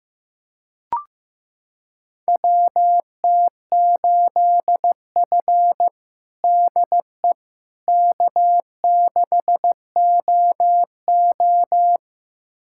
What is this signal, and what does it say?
Morse code, heard as a keyed tone, about 15 wpm, 700 Hz. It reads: WT8F DE K6OO
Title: USA Callsign DE Callsign - 4 Characters - 15wpm